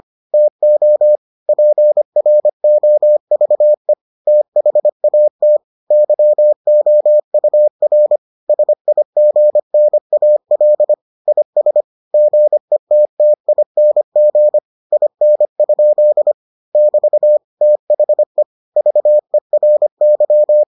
Morse code, heard as a keyed tone, about 25 wpm, 600 Hz. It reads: TO PROVE THAT YOUR SIGNAL IS GETTING IN? <BT> THE VERY